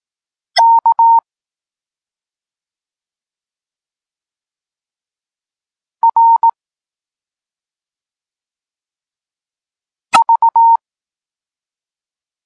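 Morse code, heard as a keyed tone, about 18 wpm, 920 Hz. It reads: K R V
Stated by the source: Morse code lesson for learning the letters K and X, from the SP9MOA Niepołomicki Klub Krótkofalowców course.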